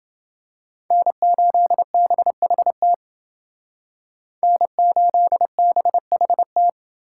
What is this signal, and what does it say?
Morse code, heard as a keyed tone, about 29 wpm, 700 Hz. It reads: N8BHT N8BHT